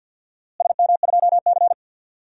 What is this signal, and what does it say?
Morse code, heard as a keyed tone, about 52 wpm, 700 Hz. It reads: SM2C
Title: DXCC-100 Call Signs - 4 Characters - Repeated 3x - 50wpm